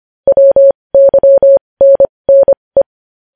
Morse code, heard as a keyed tone, about 25 wpm, spaced as for 20 wpm, 550 Hz. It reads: WYNNE
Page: news-headlines-in-morse-code-at-20-wpm